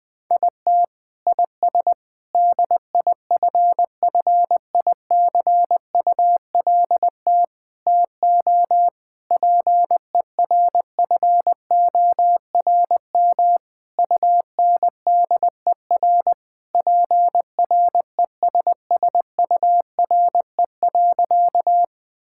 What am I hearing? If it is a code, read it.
Morse code, heard as a keyed tone, about 20 wpm, 700 Hz. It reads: IT IS DIFFICULT TO PERFORM UNDER PRESSURE.
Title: Sentences from Top 1000 Words - Review Part 6 - 20wpm